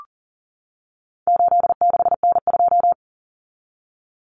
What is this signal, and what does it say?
Morse code, heard as a keyed tone, about 40 wpm, 700 Hz. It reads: 86N2
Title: Common Cut Numbers - 4-Digit 40wpm